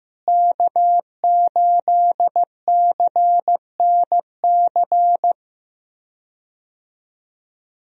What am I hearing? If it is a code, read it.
Morse code, heard as a keyed tone, about 15 wpm, 700 Hz. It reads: K8CNC
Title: USA Call Signs - 5 Characters - Repeated 3x - 15wpm